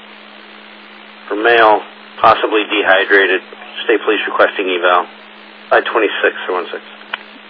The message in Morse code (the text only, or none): none